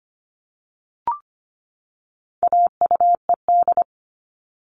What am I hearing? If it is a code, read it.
Morse code, heard as a keyed tone, about 25 wpm, 700 Hz. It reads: AUED
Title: All Cut Numbers - 4-Digit 25wpm